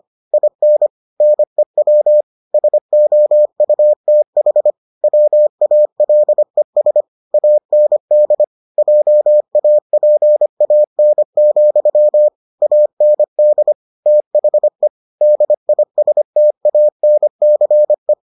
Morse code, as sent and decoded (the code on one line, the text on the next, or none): .. -. / -. .-- / ... --- ..- - .... / .-- .- .-.. ... / .- -. -.. / .--- .- .--. .- -. --..-- / .- -. -.. / - .... / -.. .. ... - .- -. -.-. .
IN NEW SOUTH WALES AND JAPAN, AND THE DISTANCE